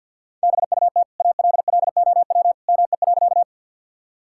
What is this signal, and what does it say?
Morse code, heard as a keyed tone, about 50 wpm, 700 Hz. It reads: BUT ALLOW ME.
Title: Sentences from Top 1000 Words - Review Part 5 - 50wpm